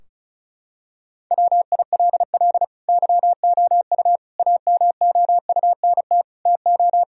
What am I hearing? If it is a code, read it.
Morse code, heard as a keyed tone, about 36 wpm, 700 Hz. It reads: WILL YOU AMOUNT TO
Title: Sentences from Top 1000 Words - Review Part 4 - 35wpm